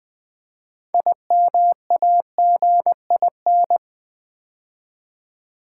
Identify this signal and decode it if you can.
Morse code, heard as a keyed tone, about 20 wpm, 700 Hz. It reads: IMAGIN
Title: Build A Word - 3 Letters At A Time - 9-Letter Words - 20wpm